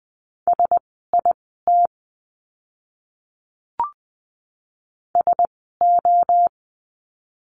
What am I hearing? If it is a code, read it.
Morse code, heard as a keyed tone, about 20 wpm, spaced as for 15 wpm, 700 Hz. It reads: SIT SO
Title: 115 Lesson 19f - Letter w - Words 20wpm